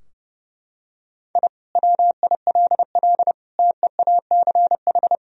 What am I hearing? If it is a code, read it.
Morse code, heard as a keyed tone, about 32 wpm, 700 Hz. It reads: I WILL TEACH
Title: Sentences from Top 1000 Words - Review Part 2 - 30wpm